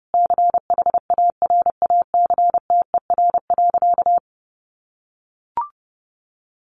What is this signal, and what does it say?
Morse code, heard as a keyed tone, about 30 wpm, 700 Hz. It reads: CHARACTER.
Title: Sentences from Top 1500 Words 30wpm